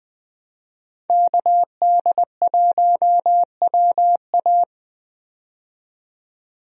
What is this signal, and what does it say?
Morse code, heard as a keyed tone, about 20 wpm, 700 Hz. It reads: KD1WA
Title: USA Call Signs - 5 Characters - 20wpm